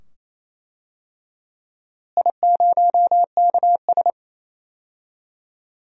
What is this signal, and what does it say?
Morse code, heard as a keyed tone, about 28 wpm, 700 Hz. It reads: I0KS
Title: DXCC-100 Call Signs - 4 Characters - Repeated 2x - 28wpm